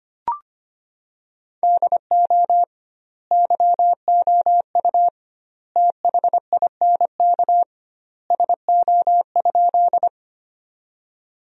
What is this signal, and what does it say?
Morse code, heard as a keyed tone, about 25 wpm, 700 Hz. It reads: DO YOU THINK SO?
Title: 2x Word-Spacing - Sentences from Top 200 Words - 25wpm